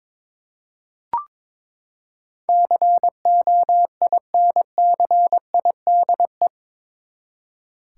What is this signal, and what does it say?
Morse code, heard as a keyed tone, about 22 wpm, 700 Hz. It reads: COINCIDE
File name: practice / 22wpm / Eight Letter Words - 22wpm